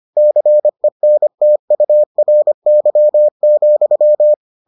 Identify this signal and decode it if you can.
Morse code, heard as a keyed tone, about 25 wpm, 600 Hz. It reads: CENTURY,